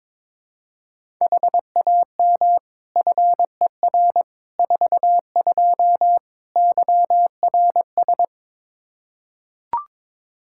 Morse code, heard as a keyed tone, about 22 wpm, 700 Hz. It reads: HAM FER 42 YRS
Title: Practice QSO - Exchange 3 - More Info - 22wpm